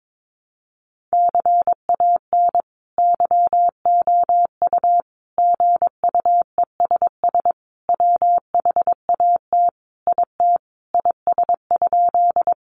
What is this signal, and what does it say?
Morse code, heard as a keyed tone, about 22 wpm, 700 Hz. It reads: CAN YOU GUESS WHAT IT IS?
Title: Sentences from Top 1500 Words 22wpm